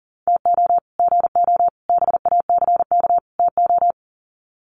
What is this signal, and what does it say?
Morse code, heard as a keyed tone, about 38 wpm, 700 Hz. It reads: TO GO BACK TO
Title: Sets of 4 Words 40wpm